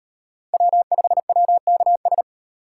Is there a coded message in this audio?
Yes: Morse code from 0.5 to 2.2 s, heard as a keyed tone, about 38 wpm, 700 Hz. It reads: W5WKS